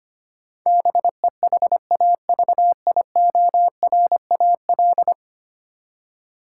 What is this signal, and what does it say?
Morse code, heard as a keyed tone, about 25 wpm, 700 Hz. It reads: BEHAVIORAL
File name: 10 Letter Words - First 3 Letters Given - 25wpm